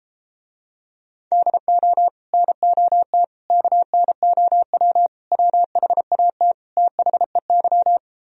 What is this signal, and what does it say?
Morse code, heard as a keyed tone, about 34 wpm, 700 Hz. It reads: DO NOT KNOW WHAT THEY